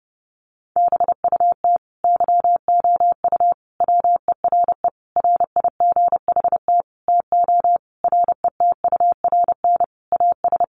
Morse code, heard as a keyed tone, about 30 wpm, 700 Hz. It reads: BUT YOU WERE RIGHT TO RETURN AS
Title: Sentences from Top 800 Words 30wpm